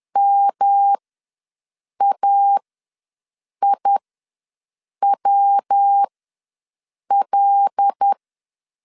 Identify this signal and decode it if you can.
Morse code, heard as a keyed tone, about 10 wpm, spaced as for 6 wpm, 800 Hz. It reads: MAIWL
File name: LEZIONI / 5 LBVC / 5 L ws 10wpm 3636